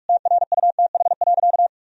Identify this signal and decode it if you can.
Morse code, heard as a keyed tone, about 46 wpm, 700 Hz. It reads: TRUTH.
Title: Sentences from Top 1000 Words - Review Part 3 - 45wpm